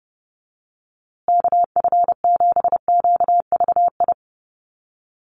Morse code, heard as a keyed tone, about 30 wpm, 700 Hz. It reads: KF7QVI
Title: USA Call Signs Repeated 3x 30wpm